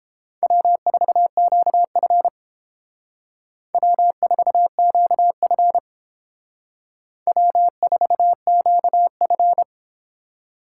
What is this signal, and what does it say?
Morse code, heard as a keyed tone, about 30 wpm, 700 Hz. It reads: W4QF W4QF W4QF